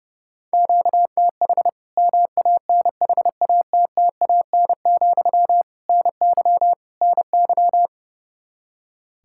Morse code, heard as a keyed tone, about 30 wpm, 700 Hz. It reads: QTH MANHATTAN, NY NY
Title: Practice QSO - Exchange 1 - Beginning - 30wpm